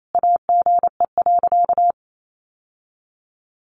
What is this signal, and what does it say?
Morse code, heard as a keyed tone, about 28 wpm, 700 Hz. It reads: AGE.
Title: Sentences from Top 1000 Words 28wpm